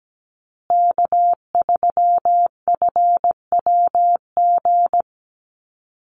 Morse code, heard as a keyed tone, about 17 wpm, 700 Hz. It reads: K3FWG
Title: USA Call Signs Repeated 3x 17wpm